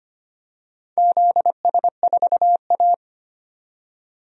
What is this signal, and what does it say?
Morse code, heard as a keyed tone, about 25 wpm, 700 Hz. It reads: ZS4A
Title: DXCC-100 Call Signs - 4 Characters - 25wpm